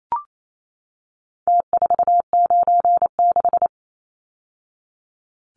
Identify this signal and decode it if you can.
Morse code, heard as a keyed tone, about 28 wpm, 700 Hz. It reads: T496
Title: Numbers - Contest Style - 4-Digit 28wpm